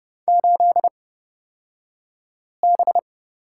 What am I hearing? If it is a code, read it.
Morse code, heard as a keyed tone, about 29 wpm, spaced as for 5 wpm, 690 Hz. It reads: 8B
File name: Lesson 36d - BT - Two Characters 30wpm